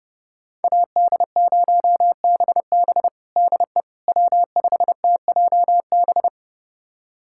Morse code, heard as a keyed tone, about 30 wpm, 700 Hz. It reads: AD0BB DE W5TJB
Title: USA Callsign DE Callsign - 5 Characters - 30wpm